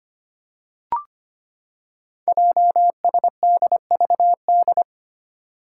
Morse code, heard as a keyed tone, about 25 wpm, 690 Hz. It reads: JSDVD
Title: Code Groups - 5 Letters - 25wpm